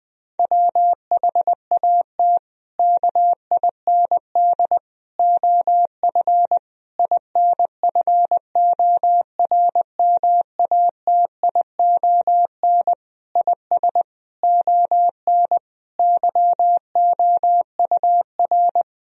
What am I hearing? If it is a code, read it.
Morse code, heard as a keyed tone, about 21 wpm, 700 Hz. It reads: WHAT KIND OF INFORMATION IS ON YOUR
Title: Sentences from Top 1000 Words - Review Part 3 - 20wpm